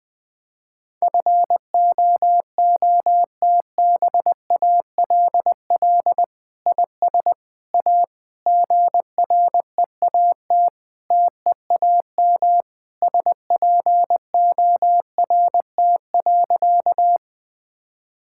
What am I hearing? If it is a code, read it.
Morse code, heard as a keyed tone, about 20 wpm, 700 Hz. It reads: FOOTBALL IS A GREAT TEAM SPORT.
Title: Sentences from Top 1600 Words 20wpm